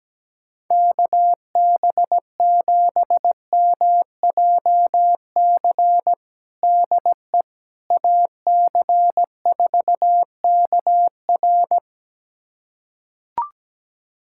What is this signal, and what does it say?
Morse code, heard as a keyed tone, about 17 wpm, 700 Hz. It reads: KB7MJC DE AC4KR
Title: USA Callsign DE Callsign 17wpm